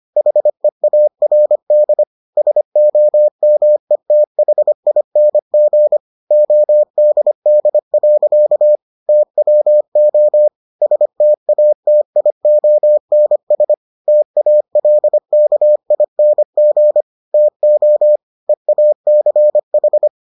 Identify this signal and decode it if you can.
Morse code, heard as a keyed tone, about 25 wpm, 600 Hz. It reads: HEARD SOMETHING ODD. TWO STATIONS TALKING TO EACH